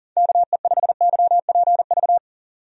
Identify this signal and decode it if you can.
Morse code, heard as a keyed tone, about 40 wpm, 700 Hz. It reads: KE5YPV
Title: USA Call Signs - 6 Characters - 40wpm